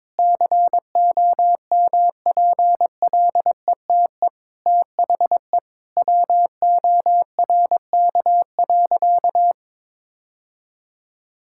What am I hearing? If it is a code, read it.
Morse code, heard as a keyed tone, about 22 wpm, 700 Hz. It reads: COMPLETE THE WORK.
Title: Sentences from Top 1000 Words - Review Part 6 - 22wpm